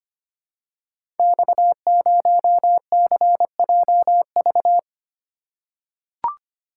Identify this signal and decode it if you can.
Morse code, heard as a keyed tone, about 25 wpm, 690 Hz. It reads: X0CJV
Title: Code Groups - 5 Letter:Numbers - 25wpm